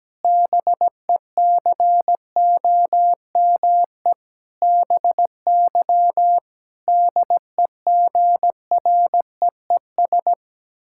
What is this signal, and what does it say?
Morse code, heard as a keyed tone, about 17 wpm, 700 Hz. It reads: BECOME BY DEGREES
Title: Pangram Challenge v2 - 17wpm